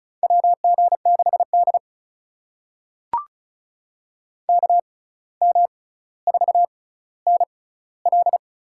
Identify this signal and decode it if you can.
Morse code, heard as a keyed tone, about 36 wpm, spaced as for 17 wpm, 700 Hz. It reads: WG6D KM4NL
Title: USA Call Signs - 6x Character Spacing - 35wpm